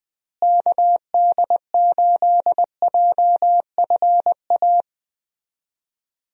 Morse code, heard as a keyed tone, about 21 wpm, 700 Hz. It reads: KD8JFA
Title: USA Call Signs - 6 Characters - Repeated 3x - 20wpm